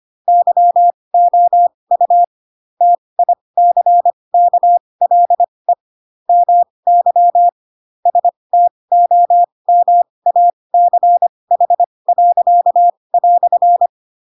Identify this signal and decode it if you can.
Morse code, heard as a keyed tone, about 25 wpm, spaced as for 20 wpm, 700 Hz. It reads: YOU TICKLE MY STOMACH."
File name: Pinocchio / Pinocchio ch1 15 25 ik1hge